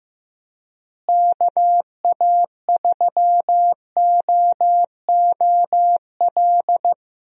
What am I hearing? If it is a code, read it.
Morse code, heard as a keyed tone, about 15 wpm, 700 Hz. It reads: KA3OOL
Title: USA Call Signs - 6 Characters - Repeated 3x - 15wpm